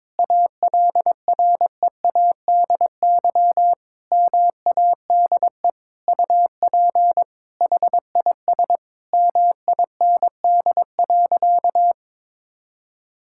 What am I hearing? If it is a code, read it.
Morse code, heard as a keyed tone, about 22 wpm, 700 Hz. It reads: ALREADY MADE UP HIS MIND.
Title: Sentences from Top 1000 Words - Review Part 5 - 22wpm